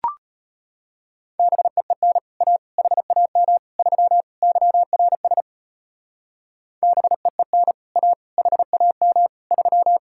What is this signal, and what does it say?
Morse code, heard as a keyed tone, about 33 wpm, 700 Hz. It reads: BEEN A HAM 3 YRS BEEN A HAM 3